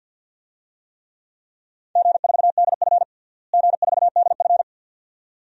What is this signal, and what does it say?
Morse code, heard as a keyed tone, about 52 wpm, 700 Hz. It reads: G4DF G4DF
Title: DXCC-100 Call Signs - 4 Characters - Repeated 2x - 50wpm